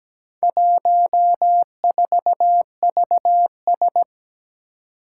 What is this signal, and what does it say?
Morse code, heard as a keyed tone, about 17 wpm, 700 Hz. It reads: 14VS
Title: Four Challenging Letter-Number Combinations 17wpm